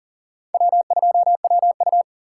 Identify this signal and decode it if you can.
Morse code, heard as a keyed tone, about 40 wpm, 700 Hz. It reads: W2WU
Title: USA Call Signs - 4 Characters - 4x Character Spacing - 40wpm